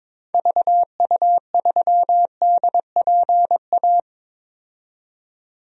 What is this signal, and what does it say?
Morse code, heard as a keyed tone, about 22 wpm, 700 Hz. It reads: VU3DPA
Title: DXCC-100 Call Signs - 6 Characters - 22wpm